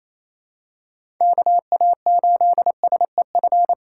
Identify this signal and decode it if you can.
Morse code, heard as a keyed tone, about 28 wpm, 700 Hz. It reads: KA8SEF
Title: USA Call Signs - 2x Character Spacing - 28wpm